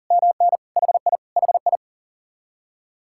Morse code, heard as a keyed tone, about 35 wpm, 700 Hz. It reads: MN HI HI
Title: Speed-Racing Common QSO Phrases - 40wpm